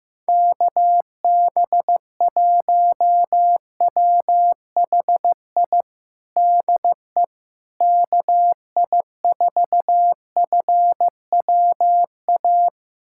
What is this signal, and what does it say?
Morse code, heard as a keyed tone, about 15 wpm, 700 Hz. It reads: KB1WHI DE KI4FWA
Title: USA Callsign DE Callsign - 6 Characters - 15wpm